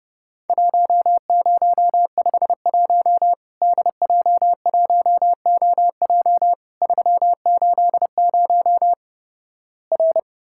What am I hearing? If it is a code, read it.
Morse code, heard as a keyed tone, about 32 wpm, 700 Hz. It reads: 1051 DJ1OJ 380 R
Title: Worked All Europe DX Contest - QTC Exchange - 30wpm